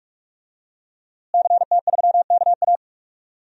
Morse code, heard as a keyed tone, about 41 wpm, 700 Hz. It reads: CT3KA
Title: DXCC-100 Call Signs - 5 Characters - Repeated 3x - 45wpm